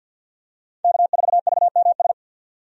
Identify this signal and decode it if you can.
Morse code, heard as a keyed tone, about 50 wpm, 700 Hz. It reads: K4VMS
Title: USA Call Signs - 5 Characters - 50wpm